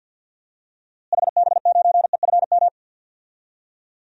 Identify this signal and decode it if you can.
Morse code, heard as a keyed tone, about 48 wpm, 700 Hz. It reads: HB9EFM
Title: DXCC-100 Call Signs - 6 Characters - Repeated 2x - 50wpm